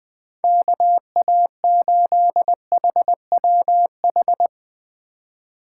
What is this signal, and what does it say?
Morse code, heard as a keyed tone, about 20 wpm, 700 Hz. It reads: KA8HWH
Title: Speed Racing - USA 2x3 Call Signs 17wpm